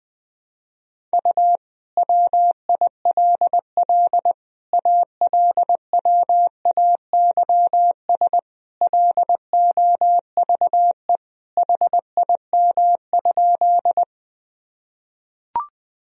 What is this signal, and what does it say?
Morse code, heard as a keyed tone, about 21 wpm, 700 Hz. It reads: U WILL ALWAYS LOVE HIM?
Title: Sentences from Top 400 Words and Abbreviations 20wpm